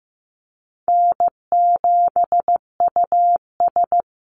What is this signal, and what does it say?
Morse code, heard as a keyed tone, about 15 wpm, 700 Hz. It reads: N7US
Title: Call Signs Repeated 3x - 15wpm